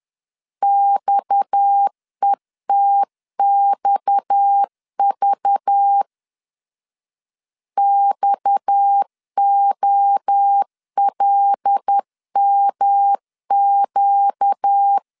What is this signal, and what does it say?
Morse code, heard as a keyed tone, about 10 wpm, 800 Hz. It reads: XETXV XOLMQ
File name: LEZIONI / 6 PKX0 / 6 PKX 10wpm 2426